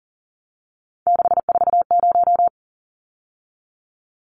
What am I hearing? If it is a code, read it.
Morse code, heard as a keyed tone, about 40 wpm, 700 Hz. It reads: 640
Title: Numbers - 3-Digit 40wpm